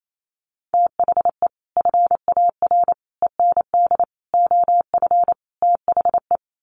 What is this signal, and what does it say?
Morse code, heard as a keyed tone, about 28 wpm, 700 Hz. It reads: THE FAR END OF THE
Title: Sets of 5 Words 28wpm